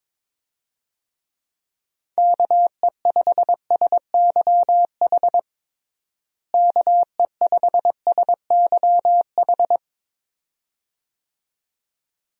Morse code, heard as a keyed tone, about 22 wpm, 700 Hz. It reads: KE5SYH KE5SYH
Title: USA Call Signs - 6 Characters - Repeated 2x - 22wpm